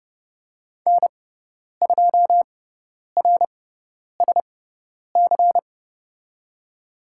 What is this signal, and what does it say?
Morse code, heard as a keyed tone, about 30 wpm, spaced as for 10 wpm, 700 Hz. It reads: N2RSC
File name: Lesson 18g - Letter m - Call Signs 30wpm